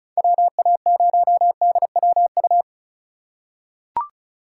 Morse code, heard as a keyed tone, about 36 wpm, 700 Hz. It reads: WA0DWU
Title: USA Call Signs - 6 Characters - 3x Character Spacing - 35wpm